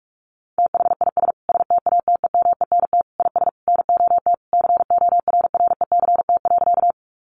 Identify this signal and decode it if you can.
Morse code, heard as a keyed tone, about 44 wpm, 700 Hz. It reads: THIS STATEMENT IS NOT CORRECT.
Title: Sentences from Top 1200 Words 45wpm